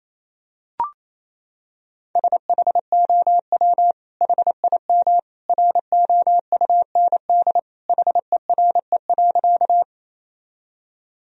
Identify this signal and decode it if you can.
Morse code, heard as a keyed tone, about 27 wpm, 700 Hz. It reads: SHOW HIM ROUND HERE.
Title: Sentences from Top 1000 Words - Review Part 1 - 28wpm